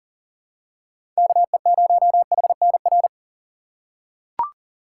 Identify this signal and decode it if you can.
Morse code, heard as a keyed tone, about 40 wpm, 700 Hz. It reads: KE0HNR
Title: USA Call Signs - 6 Characters - 6x Character Spacing - 40wpm